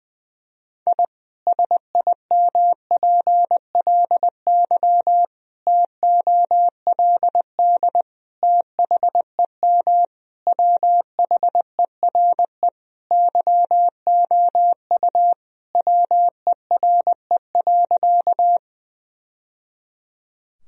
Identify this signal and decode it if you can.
Morse code, heard as a keyed tone, about 20 wpm, 700 Hz. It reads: I SIMPLY TOLD THEM WHERE YOU WERE.
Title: Sentences from Top 1000 Words - Review Part 4 - 20wpm